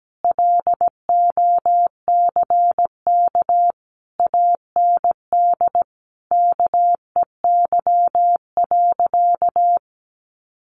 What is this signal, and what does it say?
Morse code, heard as a keyed tone, about 17 wpm, 700 Hz. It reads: LOCK AND KEY.
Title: Sentences from Top 1400 Words 17wpm